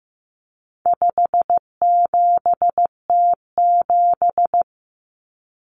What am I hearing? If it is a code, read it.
Morse code, heard as a keyed tone, about 15 wpm, 700 Hz. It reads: 57T7
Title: Common Cut Numbers - 4-Digit 15wpm